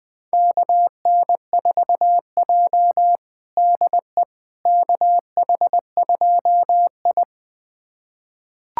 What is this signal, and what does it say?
Morse code, heard as a keyed tone, about 20 wpm, 700 Hz. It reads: KN4J DE KH2I
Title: USA Callsign DE Callsign - 4 Characters - 20wpm